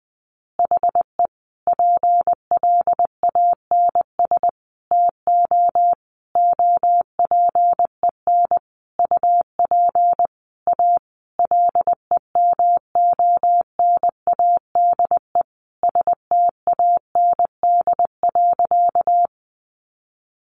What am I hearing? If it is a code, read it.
Morse code, heard as a keyed tone, about 20 wpm, 700 Hz. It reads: HE PLANS TO OPEN UP A LEMONADE STAND.
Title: Additional Easy Phrases 20wpm